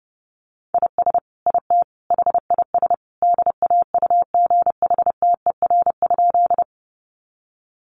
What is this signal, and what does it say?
Morse code, heard as a keyed tone, about 30 wpm, 700 Hz. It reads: IS IT HIS DAUGHTER?
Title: Sentences from Top 1500 Words 30wpm